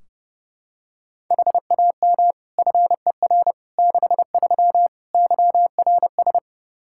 Morse code, heard as a keyed tone, about 32 wpm, 700 Hz. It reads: HAM FER 63 YRS
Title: Practice QSO - Exchange 3 - More Info - 30wpm